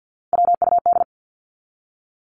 Morse code, heard as a keyed tone, about 49 wpm, 700 Hz. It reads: WUD
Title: Abbreviations 50wpm